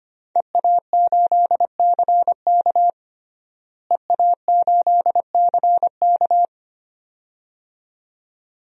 Morse code, heard as a keyed tone, about 25 wpm, 700 Hz. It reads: EA8CK EA8CK
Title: DXCC-100 Call Signs - 5 Characters - Repeated 2x - 25wpm